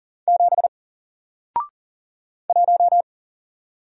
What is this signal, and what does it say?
Morse code, heard as a keyed tone, about 40 wpm, 700 Hz. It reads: 7 1
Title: Single Letter-Number Alt 40wpm